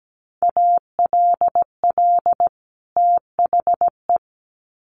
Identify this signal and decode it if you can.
Morse code, heard as a keyed tone, about 17 wpm, 700 Hz. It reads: ALL THE
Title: Sets of 2 Words 17wpm